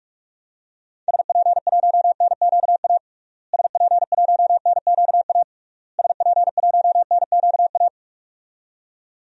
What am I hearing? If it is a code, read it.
Morse code, heard as a keyed tone, about 44 wpm, 700 Hz. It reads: SP1NQA SP1NQA SP1NQA
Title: DXCC-100 Call Signs - 6 Characters - Repeated 3x - 45wpm